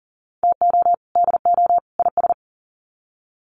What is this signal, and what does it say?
Morse code, heard as a keyed tone, about 39 wpm, 700 Hz. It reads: TO DO IS
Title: Sets of 3 Words 40wpm